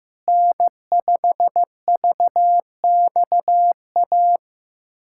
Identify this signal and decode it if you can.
Morse code, heard as a keyed tone, about 15 wpm, 700 Hz. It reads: N5VXA
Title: USA Callsign DE Callsign 15wpm